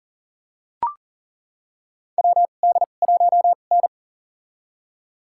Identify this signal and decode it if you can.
Morse code, heard as a keyed tone, about 40 wpm, spaced as for 29 wpm, 700 Hz. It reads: WD1N
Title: USA Call Signs - 2x Character Spacing - 40wpm